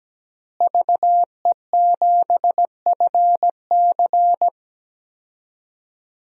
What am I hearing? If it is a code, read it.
Morse code, heard as a keyed tone, about 17 wpm, 700 Hz. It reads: VE7FC
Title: DXCC-100 Call Signs - 5 Characters - Repeated 2x - 17wpm